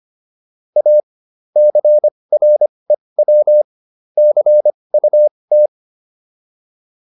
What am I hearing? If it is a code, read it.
Morse code, heard as a keyed tone, about 25 wpm, spaced as for 20 wpm, 600 Hz. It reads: A CREW CUT